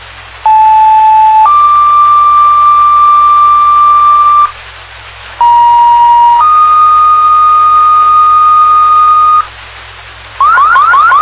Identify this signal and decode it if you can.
Two-tone page, heard as a one-second tone, then a three-second tone, about 870 then 1180 Hz, about 950 then 1230 Hz. It is heard from 0.4 to 9.4 s.